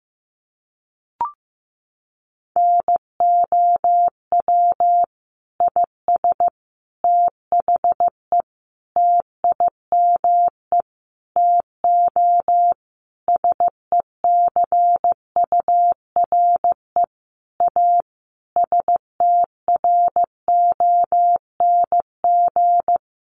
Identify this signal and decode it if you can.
Morse code, heard as a keyed tone, about 15 wpm, 700 Hz. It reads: NOW IS THE TIME TO SECURE A STRONG
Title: Sentences from Top 1200 Words 15wpm